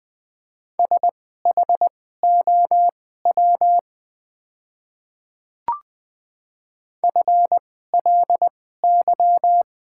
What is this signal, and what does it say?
Morse code, heard as a keyed tone, about 20 wpm, spaced as for 14 wpm, 690 Hz. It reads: SHOW FLY
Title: 2x Character Spacing - Top 500 Words - 20wpm